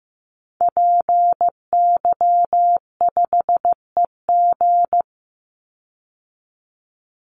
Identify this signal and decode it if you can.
Morse code, heard as a keyed tone, about 15 wpm, 700 Hz. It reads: PY5EG
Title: Call Signs Repeated 2x - 15wpm